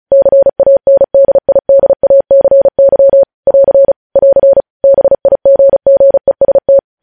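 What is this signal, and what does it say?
Morse code, heard as a keyed tone, about 36 wpm, 550 Hz. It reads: CANDIDACY <AR> <AR> BIGGEST
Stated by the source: Morse code news headlines sent at 35.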